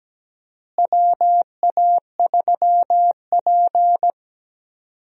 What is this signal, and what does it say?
Morse code, heard as a keyed tone, about 17 wpm, 700 Hz. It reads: WA3P
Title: USA Call Signs - 4 Characters - Repeated 3x - 17wpm